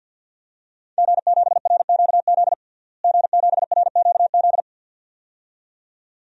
Morse code, heard as a keyed tone, about 52 wpm, 700 Hz. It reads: G7RQZ G7RQZ
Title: DXCC-100 Call Signs - 5 Characters - Repeated 2x - 50wpm